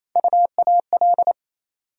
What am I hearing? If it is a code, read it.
Morse code, heard as a keyed tone, about 28 wpm, 700 Hz. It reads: UAL
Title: USA Call Sign Postfixes - 3 Letters 28wpm